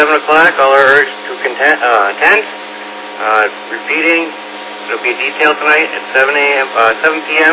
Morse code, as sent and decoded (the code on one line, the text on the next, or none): none